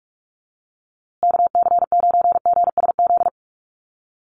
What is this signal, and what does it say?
Morse code, heard as a keyed tone, about 44 wpm, 700 Hz. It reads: KC9GSZ
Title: USA Call Signs Repeated 3x 45wpm